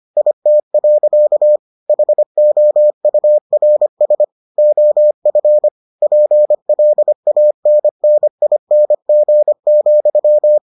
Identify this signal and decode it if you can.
Morse code, heard as a keyed tone, about 25 wpm, 600 Hz. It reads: IT. HOURS OF PLANNING,